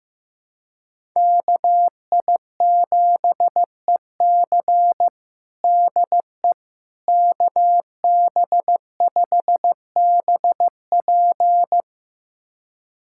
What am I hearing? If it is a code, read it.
Morse code, heard as a keyed tone, about 15 wpm, 700 Hz. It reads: KI7EC DE KB5BP
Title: USA Callsign DE Callsign - 5 Characters - 15wpm